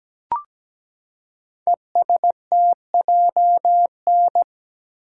Morse code, heard as a keyed tone, about 17 wpm, 700 Hz. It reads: ESTJN
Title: Code Groups - 5 Letter:Numbers - 17wpm